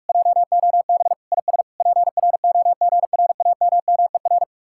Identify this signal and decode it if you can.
Morse code, heard as a keyed tone, about 50 wpm, 700 Hz. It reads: JOB IS PROGRAMMER